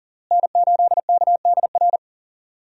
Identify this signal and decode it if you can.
Morse code, heard as a keyed tone, about 40 wpm, 700 Hz. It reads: N8KDR
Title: USA Call Signs - 5 Characters - Repeated 3x - 40wpm